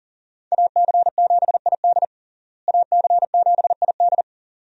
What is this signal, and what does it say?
Morse code, heard as a keyed tone, about 40 wpm, 700 Hz. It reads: AC7ID AC7ID
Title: USA Call Signs - 5 Characters - Repeated 2x - 40wpm